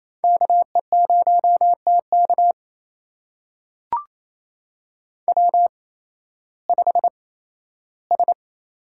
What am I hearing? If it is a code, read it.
Morse code, heard as a keyed tone, about 28 wpm, spaced as for 15 wpm, 700 Hz. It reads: KE0TK W5S
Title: USA Call Signs - 5 Characters - 8x Character Spacing - 28wpm